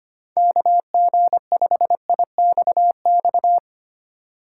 Morse code, heard as a keyed tone, about 25 wpm, 700 Hz. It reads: KG5IXX
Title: USA Call Signs - 6 Characters - 8x Character Spacing - 25wpm